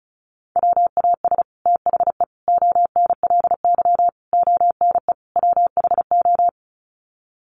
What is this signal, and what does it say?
Morse code, heard as a keyed tone, about 34 wpm, 700 Hz. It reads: WAS THE ONLY ONE WHO
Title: Sets of 5 Words Encore2 35wpm